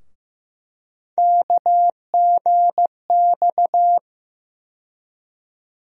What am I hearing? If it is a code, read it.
Morse code, heard as a keyed tone, about 15 wpm, 700 Hz. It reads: KGX
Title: USA Call Sign Postfixes - 3 Letters 15wpm